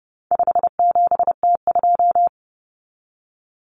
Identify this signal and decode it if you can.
Morse code, heard as a keyed tone, about 30 wpm, 700 Hz. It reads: 57T2
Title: Common Cut Numbers - 4-Digit 30wpm